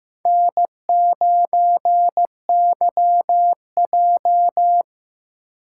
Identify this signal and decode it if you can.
Morse code, heard as a keyed tone, about 15 wpm, 700 Hz. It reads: N9YJ
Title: USA Call Signs - 4 Characters - 6x Character Spacing - 15wpm